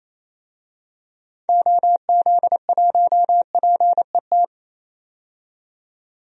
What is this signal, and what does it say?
Morse code, heard as a keyed tone, about 28 wpm, 700 Hz. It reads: OZ1PET
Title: DXCC-100 Call Signs - 6 Characters - 28wpm